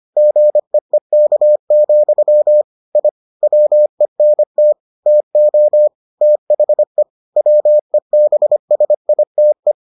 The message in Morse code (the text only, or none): GEEK, I WENT TO THE WEBSITE